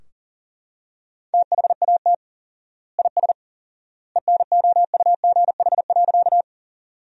Morse code, heard as a keyed tone, about 40 wpm, 700 Hz. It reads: THAT IS ENOUGH.
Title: 4x Word-Spacing - Sentences from Top 400 Words - 40wpm